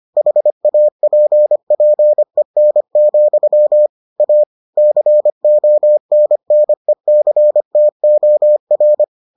Morse code, heard as a keyed tone, about 25 wpm, 600 Hz. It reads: HAPPEN, A CONNECTOR